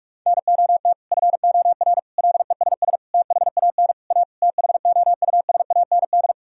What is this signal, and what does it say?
Morse code, heard as a keyed tone, about 46 wpm, 700 Hz. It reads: NOT FOR LESS THAN A THOUSAND
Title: Sentences from Top 1000 Words - Review Part 3 - 45wpm